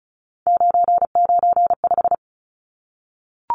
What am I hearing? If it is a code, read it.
Morse code, heard as a keyed tone, about 34 wpm, 700 Hz. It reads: 995
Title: Numbers - 3-Digit 35wpm